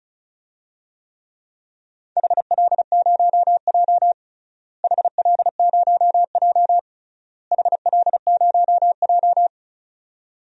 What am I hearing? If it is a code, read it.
Morse code, heard as a keyed tone, about 36 wpm, 700 Hz. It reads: HL0J HL0J HL0J
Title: DXCC-100 Call Signs - 4 Characters - Repeated 3x - 35wpm